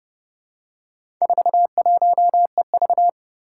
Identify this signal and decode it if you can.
Morse code, heard as a keyed tone, about 30 wpm, 700 Hz. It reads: 41EV
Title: Four Challenging Letter-Number Combinations 30wpm